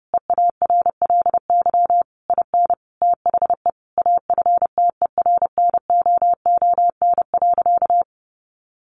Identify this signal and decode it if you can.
Morse code, heard as a keyed tone, about 30 wpm, 700 Hz. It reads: EARLY IN THE AFTERNOON.
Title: Sentences from Top 1500 Words 30wpm